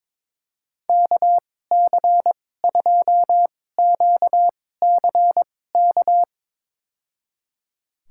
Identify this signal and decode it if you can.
Morse code, heard as a keyed tone, about 22 wpm, spaced as for 16 wpm, 700 Hz. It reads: KC2QCK
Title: USA Call Signs - 6 Characters - 2x Character Spacing - 22wpm